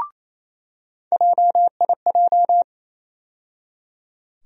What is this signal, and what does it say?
Morse code, heard as a keyed tone, about 28 wpm, 700 Hz. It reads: JIJ
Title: USA Call Sign Postfixes - 3 Letters 28wpm